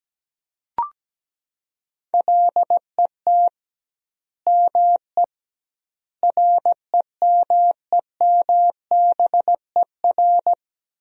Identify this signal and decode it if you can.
Morse code, heard as a keyed tone, about 17 wpm, 700 Hz. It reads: LET ME REMEMBER
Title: 2x Word-Spacing - Sentences from Top 400 Words - 17wpm